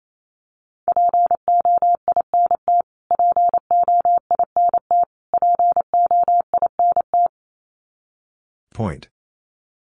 Morse code, heard as a keyed tone, about 28 wpm, 700 Hz. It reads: POINT POINT POINT
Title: Top 500 Words Repeated 28wpm